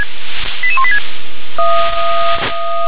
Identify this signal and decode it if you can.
Five-tone selective call (ZVEI), heard as quick runs of five tones, tones of about 70 ms. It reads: 88186